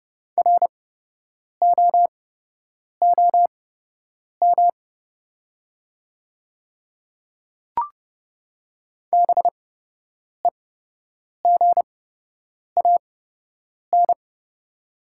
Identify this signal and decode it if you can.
Morse code, heard as a keyed tone, about 30 wpm, spaced as for 8 wpm, 700 Hz. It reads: ROOM BEGAN
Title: 8x Character Spacing - Top 500 Words - 30wpm